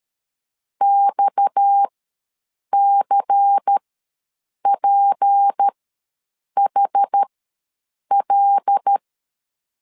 Morse code, heard as a keyed tone, about 13 wpm, spaced as for 7 wpm, 800 Hz. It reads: XCPHL
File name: LEZIONI / 6 PKX0 / 6 PKX ws 12wpm 3106